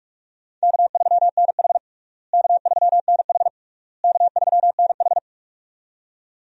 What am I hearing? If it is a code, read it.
Morse code, heard as a keyed tone, about 46 wpm, 700 Hz. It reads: K3NH K3NH K3NH
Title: USA Call Signs - 4 Characters - Repeated 3x - 45wpm